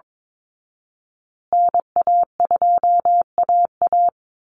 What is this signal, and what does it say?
Morse code, heard as a keyed tone, about 22 wpm, 700 Hz. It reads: NA2AA
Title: Call Signs Repeated 2x - 22wpm